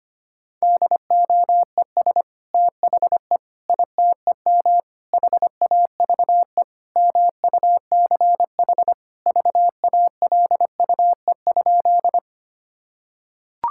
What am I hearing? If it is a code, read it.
Morse code, heard as a keyed tone, about 25 wpm, 700 Hz. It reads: DOES THE ITEM HAVE MUCH VALUE?
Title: Sentences from Top 1000 Words - Review Part 6 - 25wpm